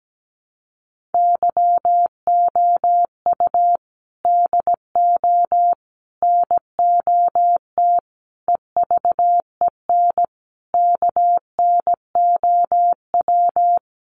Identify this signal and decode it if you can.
Morse code, heard as a keyed tone, about 17 wpm, 700 Hz. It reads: YOU DO NOT EVEN KNOW
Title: Sets of 5 Words Encore2 17wpm